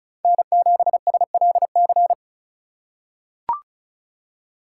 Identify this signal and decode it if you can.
Morse code, heard as a keyed tone, about 34 wpm, 690 Hz. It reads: N7SLC
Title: USA Call Signs - 4x Character Spacing - 35wpm